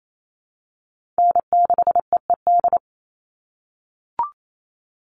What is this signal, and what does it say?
Morse code, heard as a keyed tone, about 28 wpm, 700 Hz. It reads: N6EED
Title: Call Signs - 28wpm